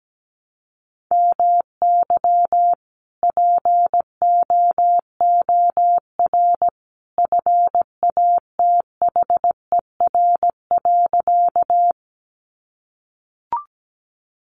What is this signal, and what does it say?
Morse code, heard as a keyed tone, about 17 wpm, 700 Hz. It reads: MY POOR FATHER.
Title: Sentences from Top 1000 Words 17wpm